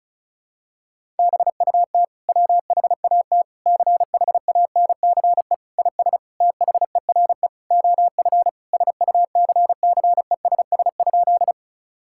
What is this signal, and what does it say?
Morse code, heard as a keyed tone, about 34 wpm, 700 Hz. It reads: BUT WHAT CHANCE IS THERE OF SUCCESS?
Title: Sentences from Top 1000 Words - Review Part 6 - 35wpm